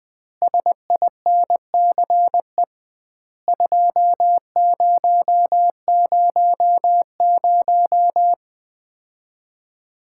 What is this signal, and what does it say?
Morse code, heard as a keyed tone, about 20 wpm, 700 Hz. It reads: SINCE 2000
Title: Common QSO Phrases - 2x Word-Spacing - 20wpm